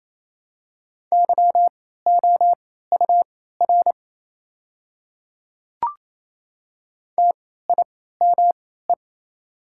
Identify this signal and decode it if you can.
Morse code, heard as a keyed tone, about 27 wpm, spaced as for 16 wpm, 700 Hz. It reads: YOUR TIME
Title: 3x Character Spacing - Top 300 Words - 28wpm